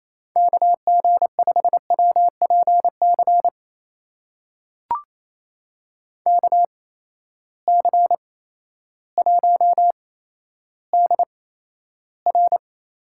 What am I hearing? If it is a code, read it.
Morse code, heard as a keyed tone, about 28 wpm, spaced as for 13 wpm, 700 Hz. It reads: KG5WPC KC1DR
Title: USA Call Signs - 6 Characters - 8x Character Spacing - 28wpm